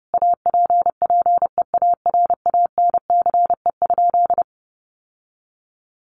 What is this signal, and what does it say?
Morse code, heard as a keyed tone, about 30 wpm, 700 Hz. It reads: APPEARANCE?
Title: Sentences from Top 1100 Words 30wpm